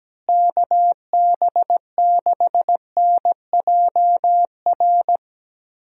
Morse code, heard as a keyed tone, about 17 wpm, 700 Hz. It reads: KB6NJR
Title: USA Call Signs - 6 Characters - Repeated 3x - 17wpm